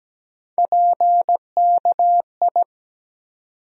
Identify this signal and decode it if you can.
Morse code, heard as a keyed tone, about 17 wpm, 700 Hz. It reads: PKI
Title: USA Call Sign Postfixes - 3 Letters 17wpm